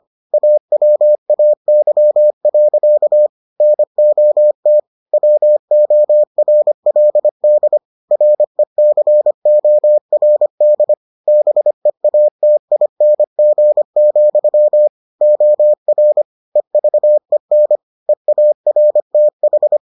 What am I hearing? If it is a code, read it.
Morse code, heard as a keyed tone, about 25 wpm, 600 Hz. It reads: AWAY. NOT WORLD RECORD BEATING, OR EVEN EARTH